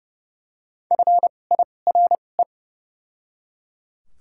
Morse code, heard as a keyed tone, about 30 wpm, spaced as for 22 wpm, 700 Hz. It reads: FIRE
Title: 2x Character Spacing - Top 500 Words - 30wpm